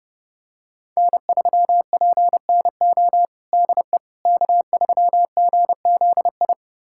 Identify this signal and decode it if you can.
Morse code, heard as a keyed tone, about 30 wpm, 700 Hz. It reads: N3PNO DE K3GZI
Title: USA Callsign DE Callsign - 5 Characters - 30wpm